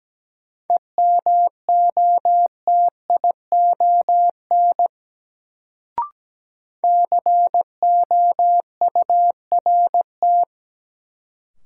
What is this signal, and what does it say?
Morse code, heard as a keyed tone, about 17 wpm, 700 Hz. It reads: EMOTION COURT
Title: Sentences from Top 1600 Words - New Word Drill Alt - 17wpm